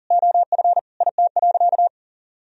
Morse code, heard as a keyed tone, about 37 wpm, 700 Hz. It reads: OF IT.
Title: Sentences from Top 200 Words and Abbreviations 40wpm